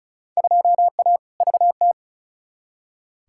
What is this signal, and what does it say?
Morse code, heard as a keyed tone, about 36 wpm, 700 Hz. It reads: 2A VT
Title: Field Day Exchanges Encore - 35wpm